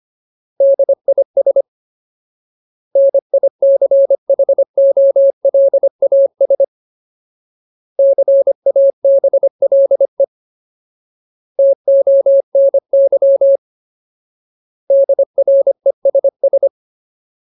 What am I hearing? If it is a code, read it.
Morse code, heard as a keyed tone, about 25 wpm, 550 Hz. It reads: DIS NICHOLAS CABLE TONY DRESS
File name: Ultimate Word List / 25WPM Ultimate